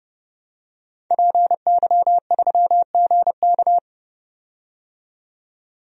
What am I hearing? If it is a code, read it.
Morse code, heard as a keyed tone, about 32 wpm, 700 Hz. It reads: PY3GK
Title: DXCC-100 Call Signs - 5 Characters - 30wpm